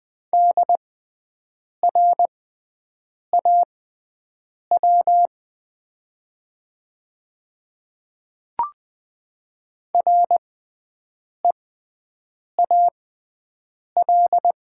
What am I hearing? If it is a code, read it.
Morse code, heard as a keyed tone, about 20 wpm, spaced as for 7 wpm, 700 Hz. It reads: DRAW REAL
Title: 6x Character Spacing - Top 300 Words - 20wpm